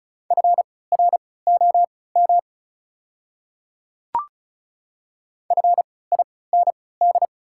Morse code, heard as a keyed tone, about 36 wpm, spaced as for 20 wpm, 690 Hz. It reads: FROM FIND
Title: 3x Character Spacing - Top 100 Words - 35wpm